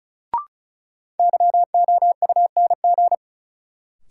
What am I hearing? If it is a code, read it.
Morse code, heard as a keyed tone, about 34 wpm, 700 Hz. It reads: YOUNG
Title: Sentences from Top 400 Words - New Word Drill - 35wpm